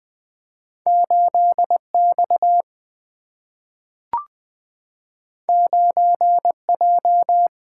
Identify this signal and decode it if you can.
Morse code, heard as a keyed tone, about 20 wpm, 700 Hz. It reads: 8 X 9 J
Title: Single-Letter Single-Number 20wpm